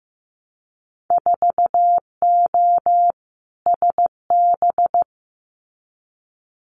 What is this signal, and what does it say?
Morse code, heard as a keyed tone, about 15 wpm, 700 Hz. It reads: 4O SB
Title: Winter Field Day Exchanges 15wpm